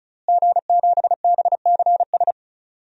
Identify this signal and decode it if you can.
Morse code, heard as a keyed tone, about 34 wpm, 700 Hz. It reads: G7BCS
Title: DXCC-100 Call Signs - 5 Characters - Repeated 3x - 35wpm